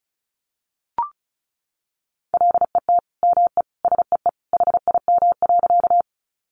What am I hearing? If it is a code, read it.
Morse code, heard as a keyed tone, about 34 wpm, 700 Hz. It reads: LET ME SEE HIM.
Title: Sentences from Top 300 Words 35wpm